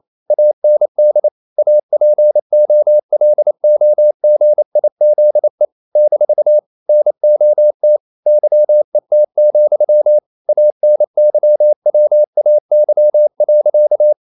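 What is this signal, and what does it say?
Morse code, heard as a keyed tone, about 28 wpm, 600 Hz. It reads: AND APOLOGIZE - NOT YET, ANYWAY.
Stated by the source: Morse code practice at 28 wpm, text of Ken Follett's novel Never in English, from CW med Gnister.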